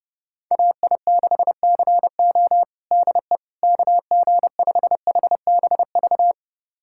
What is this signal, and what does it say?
Morse code, heard as a keyed tone, about 30 wpm, 700 Hz. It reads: AI6CO DE KG5HBV
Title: USA Callsign DE Callsign 30wpm